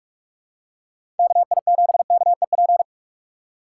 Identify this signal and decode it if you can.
Morse code, heard as a keyed tone, about 42 wpm, 700 Hz. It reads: KI7KEP
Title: USA Call Signs - 4x Character Spacing - 45wpm